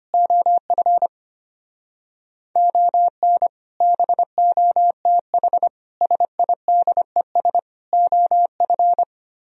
Morse code, heard as a keyed tone, about 32 wpm, 700 Hz. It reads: OF ON BOTH SIDES OF